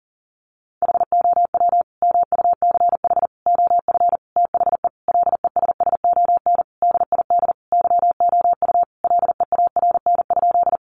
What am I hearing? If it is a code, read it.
Morse code, heard as a keyed tone, about 40 wpm, 700 Hz. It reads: HOW MUCH OF THE LESSON DID YOU LEARN?
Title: Sentences from Top 1200 Words 40wpm